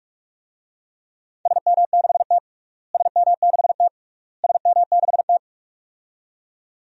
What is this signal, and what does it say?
Morse code, heard as a keyed tone, about 46 wpm, 700 Hz. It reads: SM6T SM6T SM6T
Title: DXCC-100 Call Signs - 4 Characters - Repeated 3x - 45wpm